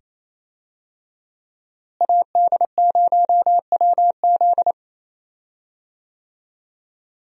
Morse code, heard as a keyed tone, about 28 wpm, 700 Hz. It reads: AD0WZ
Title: USA Call Signs - 5 Characters - Repeated 2x - 28wpm